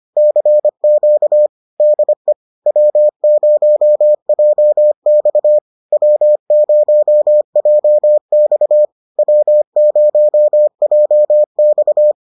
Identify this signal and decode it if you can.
Morse code, heard as a keyed tone, about 25 wpm, 600 Hz. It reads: CQ DE W0JX W0JX W0JX